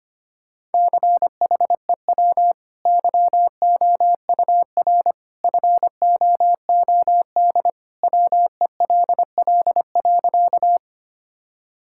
Morse code, heard as a keyed tone, about 25 wpm, 700 Hz. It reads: CHEW YOUR FOOD WELL.
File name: Easy Sentences with Hints - 25wpm